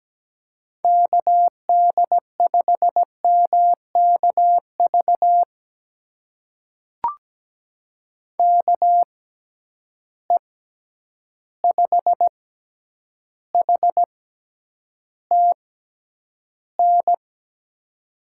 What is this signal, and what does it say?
Morse code, heard as a keyed tone, about 17 wpm, spaced as for 9 wpm, 700 Hz. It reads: KD5MKV KE5HTN
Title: USA Call Signs - 6 Characters - 6x Character Spacing - 17wpm